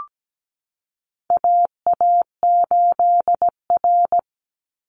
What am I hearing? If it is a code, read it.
Morse code, heard as a keyed tone, about 17 wpm, 700 Hz. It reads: AA8R
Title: Call Signs Repeated 3x - 17wpm